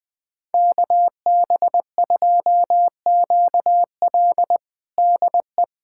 Morse code, heard as a keyed tone, about 20 wpm, 700 Hz. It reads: KB2QL DE
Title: USA Callsign DE Callsign - 5 Characters - 20wpm